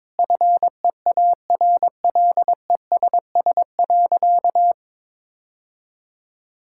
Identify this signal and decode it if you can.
Morse code, heard as a keyed tone, about 21 wpm, 700 Hz. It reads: FEARLESS.
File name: practice / 22wpm / Sentences with 4-Word Phrases - 22wpm